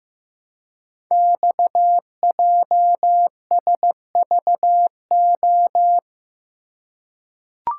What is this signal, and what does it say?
Morse code, heard as a keyed tone, about 15 wpm, 700 Hz. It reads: XJSVO
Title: Code Groups - 5 Letters - 15wpm